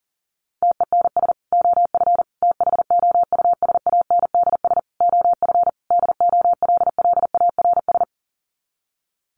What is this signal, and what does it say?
Morse code, heard as a keyed tone, about 40 wpm, 700 Hz. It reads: TENS OF THOUSANDS OF DOLLARS
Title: Sets of 5 Words Encore2 40wpm